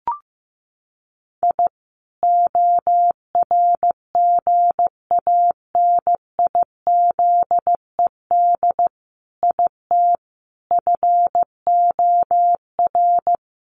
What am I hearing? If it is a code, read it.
Morse code, heard as a keyed tone, about 15 wpm, 700 Hz. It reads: I ORGANIZED IT FOR
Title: Sentences from Top 1300 Words 15wpm